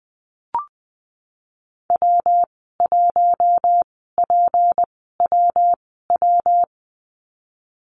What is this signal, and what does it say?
Morse code, heard as a keyed tone, about 20 wpm, spaced as for 15 wpm, 700 Hz. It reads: W1PWW
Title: 173 Lesson 28g - Number 9 - Call Signs 20wpm